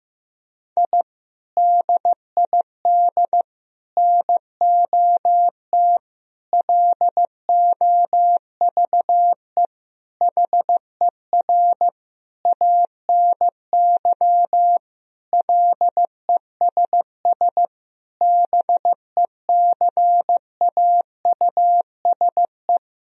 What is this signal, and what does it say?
Morse code, heard as a keyed tone, about 15 wpm, 700 Hz. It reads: I DID NOT LOVE HER ANY LESS BECAUSE
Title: Sentences from Top 1000 Words - Review Part 2 - 15wpm